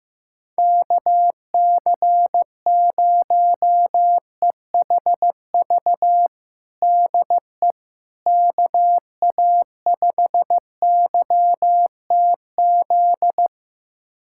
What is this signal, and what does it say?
Morse code, heard as a keyed tone, about 16 wpm, 700 Hz. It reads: KC0EHV DE KA5YTZ
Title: USA Callsign DE Callsign - 6 Characters - 15wpm